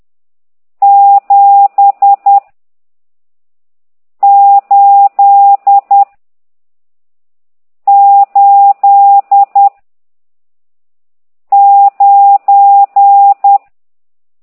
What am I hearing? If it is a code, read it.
Morse code, heard as a keyed tone, about 10 wpm, 800 Hz. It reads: 7 8 8 9
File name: ct2idw / exerc14